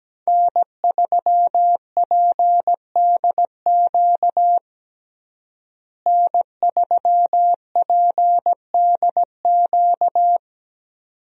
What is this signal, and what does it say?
Morse code, heard as a keyed tone, about 17 wpm, 700 Hz. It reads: N3PDQ N3PDQ
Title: USA Call Signs - 5 Characters - Repeated 2x - 17wpm